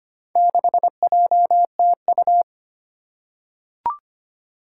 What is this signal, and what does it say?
Morse code, heard as a keyed tone, about 25 wpm, 700 Hz. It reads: 6JTU
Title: Four Challenging Letter-Number Combinations 25wpm